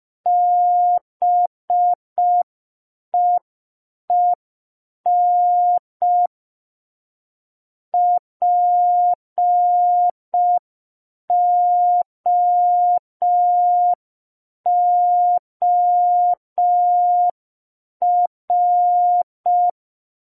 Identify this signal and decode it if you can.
Morse code, heard as a keyed tone, about 5 wpm, 710 Hz. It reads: BEEN POOR